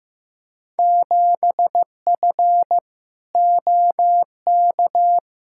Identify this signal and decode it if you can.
Morse code, heard as a keyed tone, about 15 wpm, 700 Hz. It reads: 7F OK
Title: Field Day Exchanges Encore - 15wpm